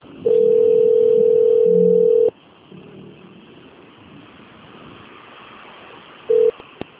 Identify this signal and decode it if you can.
Ringback tone (North America), 440 + 480 Hz.